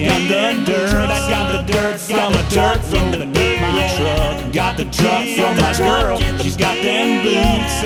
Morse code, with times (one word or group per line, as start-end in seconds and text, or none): none